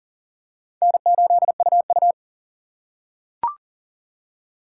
Morse code, heard as a keyed tone, about 40 wpm, 700 Hz. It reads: N8UU
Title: USA Call Signs - 4 Characters - 4x Character Spacing - 40wpm